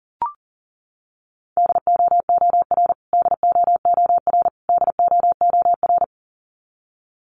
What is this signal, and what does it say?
Morse code, heard as a keyed tone, about 40 wpm, 700 Hz. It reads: DOOR DOOR DOOR
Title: Top 500 Words Repeated 40wpm